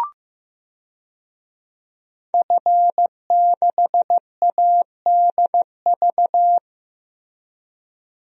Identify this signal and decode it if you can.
Morse code, heard as a keyed tone, about 15 wpm, 700 Hz. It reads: F6ADV
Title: DXCC-100 Call Signs - 5 Characters - Repeated 3x - 15wpm